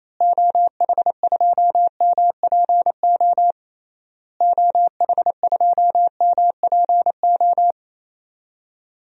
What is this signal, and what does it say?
Morse code, heard as a keyed tone, about 28 wpm, 700 Hz. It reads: OH2MPO OH2MPO
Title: DXCC-100 Call Signs - 6 Characters - Repeated 3x - 28wpm